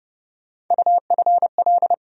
Morse code, heard as a keyed tone, about 30 wpm, 700 Hz. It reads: UFL
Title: USA Call Sign Postfixes - 3 Letters 30wpm